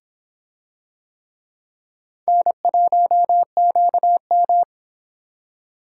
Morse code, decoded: N1QM

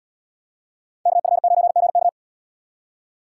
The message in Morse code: KF9GK

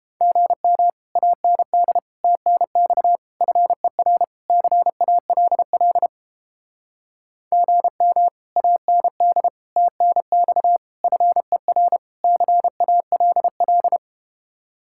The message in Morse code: GM AND TNX FER CALL GM AND TNX FER CALL

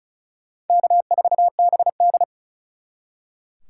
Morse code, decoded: K4BD

